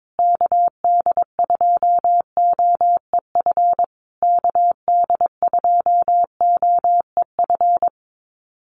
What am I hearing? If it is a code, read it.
Morse code, heard as a keyed tone, about 22 wpm, 700 Hz. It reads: KD2OEF KD2OEF
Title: USA Call Signs Repeated 3x 22wpm